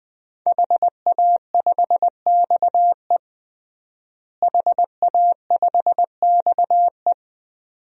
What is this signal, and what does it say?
Morse code, heard as a keyed tone, about 21 wpm, 700 Hz. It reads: HA5XE HA5XE
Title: DXCC-100 Call Signs - 5 Characters - Repeated 2x - 20wpm